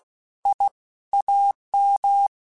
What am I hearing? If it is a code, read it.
Morse code, heard as a keyed tone, about 16 wpm, 800 Hz. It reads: I AM